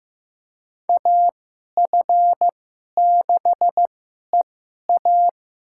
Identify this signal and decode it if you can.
Morse code, heard as a keyed tone, about 15 wpm, spaced as for 11 wpm, 700 Hz. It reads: AF6EA
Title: USA Call Signs - 5 Characters - 2x Character Spacing - 15wpm